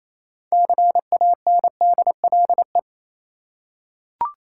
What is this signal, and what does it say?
Morse code, heard as a keyed tone, about 27 wpm, 690 Hz. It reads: CANDLE
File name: Easy Words - Speed-Racing with Hints - 22wpm